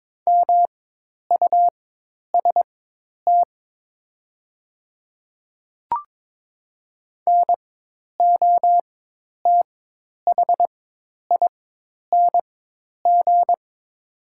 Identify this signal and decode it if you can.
Morse code, heard as a keyed tone, about 23 wpm, spaced as for 10 wpm, 700 Hz. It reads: MUST NOTHING
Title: 4x Character Spacing - Top 500 Words - 22wpm